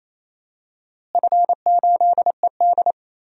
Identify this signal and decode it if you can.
Morse code, heard as a keyed tone, about 28 wpm, 700 Hz. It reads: F8ED
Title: DXCC-100 Call Signs - 4 Characters - Repeated 3x - 28wpm